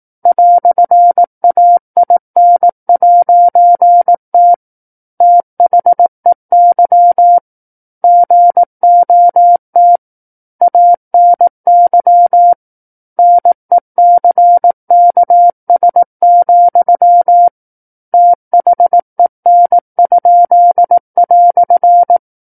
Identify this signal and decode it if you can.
Morse code, heard as a keyed tone, about 18 wpm, 700 Hz. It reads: "AIN'T THEY GOT ANY NECKS, THEN?"
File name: 18 WPM